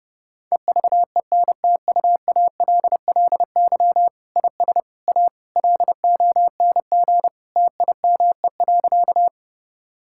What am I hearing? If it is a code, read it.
Morse code, heard as a keyed tone, about 30 wpm, 700 Hz. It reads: EVENTUALLY IS A LONG TIME.
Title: Sentences from Top 1000 Words - Review Part 6 - 30wpm